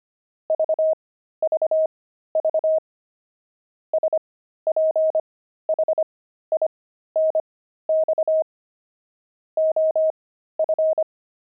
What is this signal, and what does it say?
Morse code, heard as a keyed tone, about 25 wpm, spaced as for 13 wpm, 630 Hz. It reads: VVV SPHINX OF
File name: pangram-fs-13-cs-25-sphinx